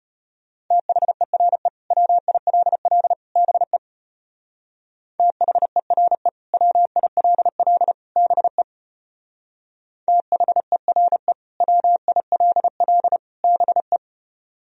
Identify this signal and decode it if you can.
Morse code, heard as a keyed tone, about 33 wpm, 700 Hz. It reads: THERE WILL BE THERE WILL BE THERE WILL BE